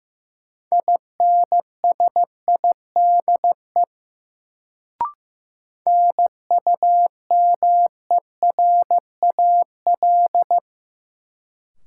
Morse code, heard as a keyed tone, about 15 wpm, 700 Hz. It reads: INSIDE NUMERAL
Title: Sentences from Top 1000 Words - Review Part 2 - New Word Drill - 15wpm